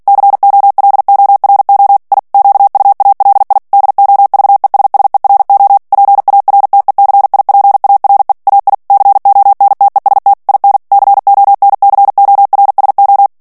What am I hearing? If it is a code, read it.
Morse code, heard as a keyed tone, about 48 wpm, 800 Hz. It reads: COLORO I QUALI DOVESSERO PARTECIPARE AI CONTEST IN CONCORSO
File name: ITALIEN texte clair 2 vitesse 45 mots minute